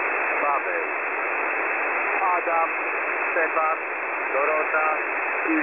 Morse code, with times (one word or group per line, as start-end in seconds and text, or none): none